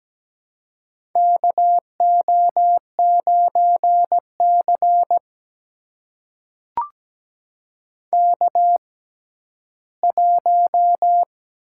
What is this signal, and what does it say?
Morse code, heard as a keyed tone, about 17 wpm, spaced as for 11 wpm, 700 Hz. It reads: KO9C K1
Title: USA Call Signs - 4 Characters - 6x Character Spacing - 17wpm